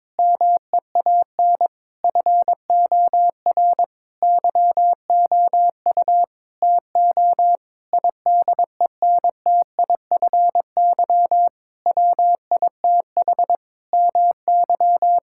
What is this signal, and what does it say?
Morse code, heard as a keyed tone, about 22 wpm, 700 Hz. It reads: MEAN FOR YOU TO IDENTIFY WITH MY
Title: Sentences from Top 1000 Words - Review Part 6 - 22wpm